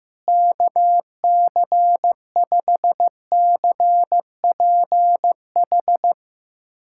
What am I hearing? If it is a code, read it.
Morse code, heard as a keyed tone, about 15 wpm, 700 Hz. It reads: KC5CPH
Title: USA Callsign DE Callsign - 6 Characters - 15wpm